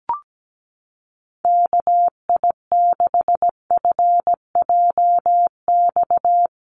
Morse code, heard as a keyed tone, about 17 wpm, 700 Hz. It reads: KI6FJX
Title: USA Call Signs Repeated 3x 17wpm